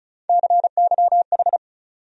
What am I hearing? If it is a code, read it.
Morse code, heard as a keyed tone, about 34 wpm, 700 Hz. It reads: CYH
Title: USA Call Sign Postfixes - 3 Letters 35wpm